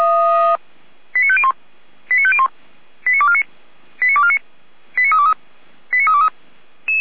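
Five-tone selective call (ZVEI), heard as quick runs of five tones, tones of about 70 ms. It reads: 89501 89501 89249 89249 89323 89323
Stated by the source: Five-tone ZVEI selective call received by SDR, from the Feuerwehr Orlamünde fire brigade audio archive.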